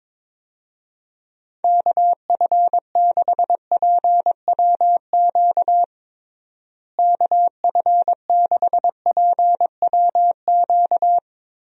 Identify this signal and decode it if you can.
Morse code, heard as a keyed tone, about 22 wpm, 700 Hz. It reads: KF6PWQ KF6PWQ
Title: USA Call Signs - 6 Characters - Repeated 3x - 22wpm